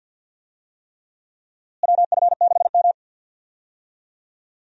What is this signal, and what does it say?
Morse code, heard as a keyed tone, about 51 wpm, 700 Hz. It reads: WF6M
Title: USA Call Signs - 4 Characters - Repeated 3x - 50wpm